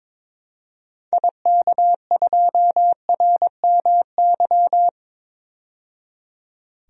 Morse code, heard as a keyed tone, about 21 wpm, 700 Hz. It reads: IK2RMY
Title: DXCC-100 Call Signs - 6 Characters - 22wpm